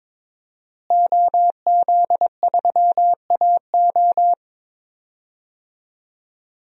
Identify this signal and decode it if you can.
Morse code, heard as a keyed tone, about 22 wpm, 700 Hz. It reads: OZ3AO
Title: DXCC-100 Call Signs - 5 Characters - Repeated 2x - 22wpm